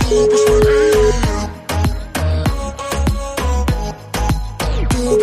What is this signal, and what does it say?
Ringback tone (Europe, 425 Hz), with noise.